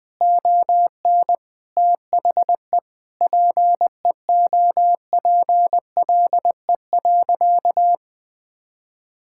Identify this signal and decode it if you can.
Morse code, heard as a keyed tone, about 20 wpm, 700 Hz. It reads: ON THE PEOPLE.
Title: Sentences from Top 1000 Words - Review Part 5 - 20wpm